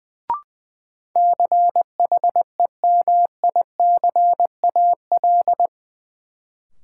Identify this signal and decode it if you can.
Morse code, heard as a keyed tone, about 20 wpm, 700 Hz. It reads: CHEMICAL
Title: Sentences from Top 1200 Words - New Word Drill - 20wpm